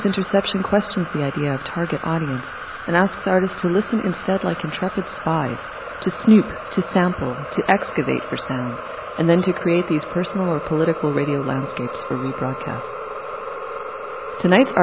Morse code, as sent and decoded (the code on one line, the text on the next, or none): none